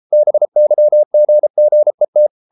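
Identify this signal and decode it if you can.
Morse code, heard as a keyed tone, about 33 wpm, 600 Hz. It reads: BYGGET
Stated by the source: Morse code at 33 wpm, QRQ.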